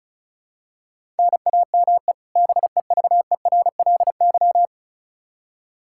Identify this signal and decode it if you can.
Morse code, heard as a keyed tone, about 36 wpm, 700 Hz. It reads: NAME BEVERLY
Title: Practice QSO - Exchange 1 - Beginning - 35wpm